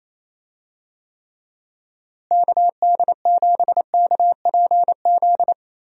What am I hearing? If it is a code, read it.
Morse code, heard as a keyed tone, about 27 wpm, 700 Hz. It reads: KD7KPZ
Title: USA Call Signs - 6 Characters - Repeated 2x - 28wpm